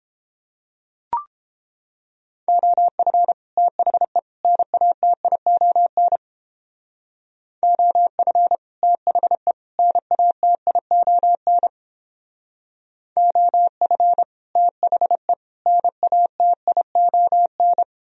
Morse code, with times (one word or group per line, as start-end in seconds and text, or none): 2.5-3.3 OF
3.6-4.2 THE
4.4-6.2 NATION
7.6-8.6 OF
8.8-9.5 THE
9.8-11.7 NATION
13.2-14.2 OF
14.6-15.3 THE
15.7-17.8 NATION